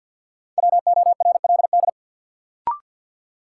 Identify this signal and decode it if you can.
Morse code, heard as a keyed tone, about 53 wpm, 700 Hz. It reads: WORLD
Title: Sets of 2 Words - New Word Drill - 50wpm